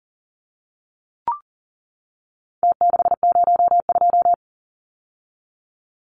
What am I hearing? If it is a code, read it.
Morse code, heard as a keyed tone, about 40 wpm, 700 Hz. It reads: T602
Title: Numbers - Contest Style - 4-Digit 40wpm